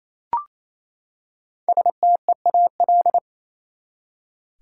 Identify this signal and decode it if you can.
Morse code, heard as a keyed tone, about 28 wpm, 700 Hz. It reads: STEAL
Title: Five Letter Words - 28wpm